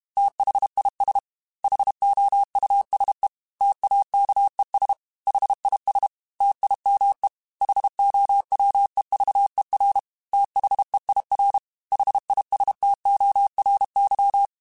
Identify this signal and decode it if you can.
Morse code, heard as a keyed tone, about 31 wpm, 800 Hz. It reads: THIS HOUSE TAKES HIS TIME HOWEVER THEIR HISTORY